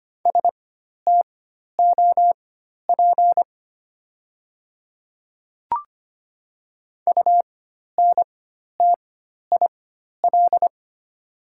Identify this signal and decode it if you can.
Morse code, heard as a keyed tone, about 25 wpm, spaced as for 12 wpm, 690 Hz. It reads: STOP UNTIL
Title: 4x Character Spacing - Top 300 Words - 25wpm